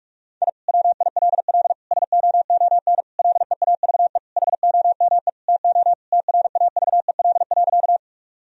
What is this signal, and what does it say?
Morse code, heard as a keyed tone, about 47 wpm, 700 Hz. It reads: I WILL SOON LEAVE HOME TO TRAVEL.
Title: Sentences from Top 1000 Words - Review Part 2 - 45wpm